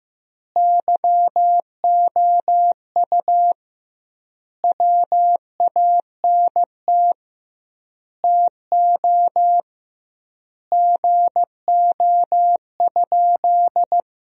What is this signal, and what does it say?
Morse code, heard as a keyed tone, about 15 wpm, 700 Hz. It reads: YOU WANT TO GO?
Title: 2x Word-Spacing - Sentences from Top 300 Words - 15wpm